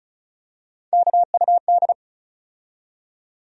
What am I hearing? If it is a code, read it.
Morse code, heard as a keyed tone, about 36 wpm, 700 Hz. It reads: KUD
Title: USA Call Sign Postfixes - 3 Letters 35wpm